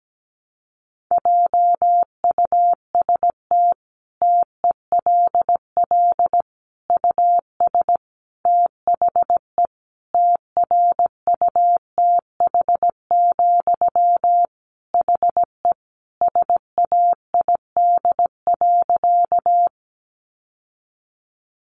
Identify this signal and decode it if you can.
Morse code, heard as a keyed tone, about 17 wpm, 700 Hz. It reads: JUST TELL US THE TRUTH, HE SAID.
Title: Sentences from Top 800 Words 17wpm